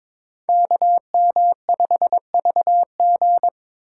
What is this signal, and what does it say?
Morse code, heard as a keyed tone, about 26 wpm, 700 Hz. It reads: KM5VG
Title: Speed Racing - USA 2x2 Call Signs 22wpm